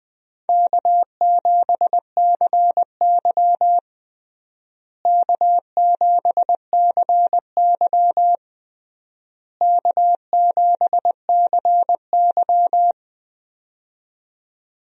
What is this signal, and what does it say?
Morse code, heard as a keyed tone, about 20 wpm, 700 Hz. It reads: K7CY K7CY K7CY
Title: USA Call Signs - 4 Characters - Repeated 3x - 20wpm